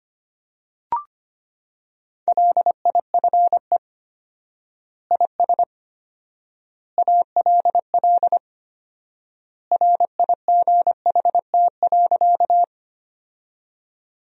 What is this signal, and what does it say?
Morse code, heard as a keyed tone, about 25 wpm, 700 Hz. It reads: LIFE IS ALL RIGHT.
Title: 4x Word-Spacing - Sentences from Top 300 Words - 25wpm